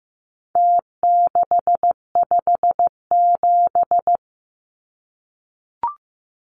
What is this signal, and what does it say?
Morse code, heard as a keyed tone, about 15 wpm, 700 Hz. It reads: T657
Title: Numbers - Contest Style - 4-Digit 15wpm